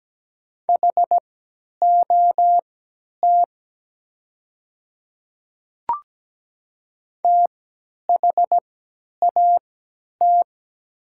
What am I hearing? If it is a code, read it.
Morse code, heard as a keyed tone, about 17 wpm, spaced as for 10 wpm, 700 Hz. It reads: HOT THAT
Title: 3x Character Spacing - Top 200 Words - 17wpm